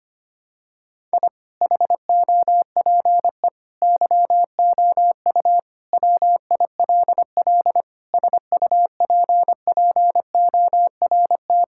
Morse code, heard as a keyed tone, about 25 wpm, 700 Hz. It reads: I HOPE YOU WILL SUPPORT
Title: Sentences from Top 1000 Words - Review Part 5 - 25wpm